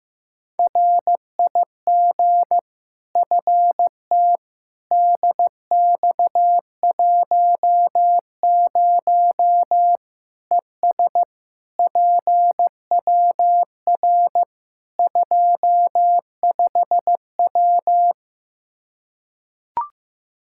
Morse code, heard as a keyed tone, about 15 wpm, 700 Hz. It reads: RIG FT DX10 ES PWR 25W
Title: Practice QSO - Exchange 2 - Info - 15wpm